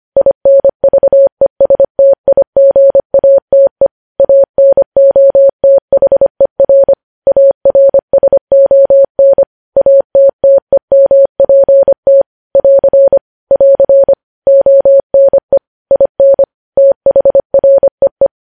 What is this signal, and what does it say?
Morse code, heard as a keyed tone, about 25 wpm, 550 Hz. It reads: INVESTIGATE ANOTHER ARSON ATTEMPT <AR> <AR> ONE IN THREE
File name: podcast / News headlines-1x-25x25